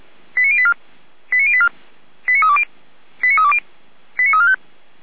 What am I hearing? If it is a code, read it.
Five-tone selective call (ZVEI), heard as quick runs of five tones, tones of about 80 ms. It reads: 89084 89084 89320 89320 89345
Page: SDR/ZVEI